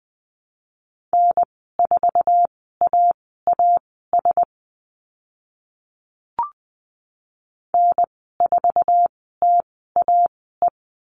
Morse code, heard as a keyed tone, about 20 wpm, spaced as for 14 wpm, 700 Hz. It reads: N4AAS N4TAE
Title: 055 Lesson 10g - Letter r - Call Signs 20wpm